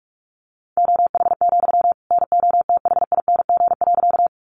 Morse code, heard as a keyed tone, about 44 wpm, 700 Hz. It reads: OH, NOTHING.
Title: Sentences from Top 500 Words 45wpm